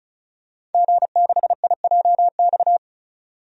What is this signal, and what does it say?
Morse code, heard as a keyed tone, about 36 wpm, 700 Hz. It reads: G6IJX
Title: DXCC-100 Call Signs - 5 Characters - 35wpm